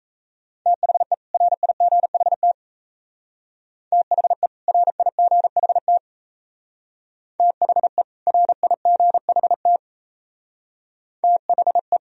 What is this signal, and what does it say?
Morse code, heard as a keyed tone, about 34 wpm, 700 Hz. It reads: THE RIGHT THE RIGHT THE RIGHT THE
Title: Speed-Racing - Sets of 2 Words 28wpm